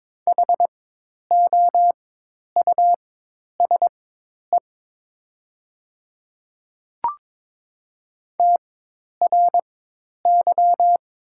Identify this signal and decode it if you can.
Morse code, heard as a keyed tone, about 22 wpm, spaced as for 10 wpm, 700 Hz. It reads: HOUSE TRY
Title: 4x Character Spacing - Top 200 Words - 22wpm